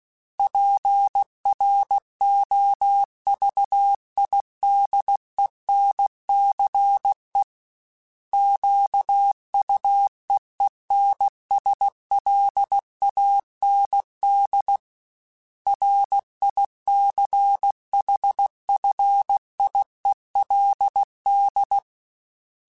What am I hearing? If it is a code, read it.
Morse code, heard as a keyed tone, about 16 wpm, 800 Hz. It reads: PROVIDENCE QUEENSLAND RICHFIELD
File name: fromSMs / FromSMs mp3s / Cities@15